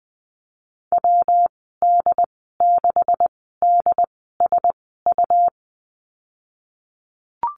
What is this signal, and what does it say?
Morse code, heard as a keyed tone, about 20 wpm, spaced as for 15 wpm, 700 Hz. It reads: WD6DSU
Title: 147 Lesson 24g - Letter y - Call Signs 20wpm